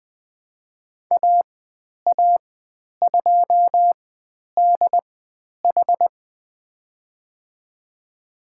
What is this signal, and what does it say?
Morse code, heard as a keyed tone, about 20 wpm, spaced as for 10 wpm, 700 Hz. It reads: AA2DH
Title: 109 Lesson 18g - Letter m - Call Signs 20wpm